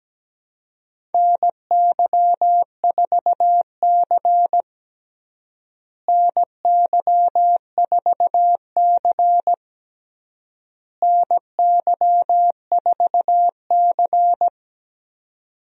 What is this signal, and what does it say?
Morse code, heard as a keyed tone, about 17 wpm, 700 Hz. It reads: NY4C NY4C NY4C
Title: USA Call Signs - 4 Characters - Repeated 3x - 17wpm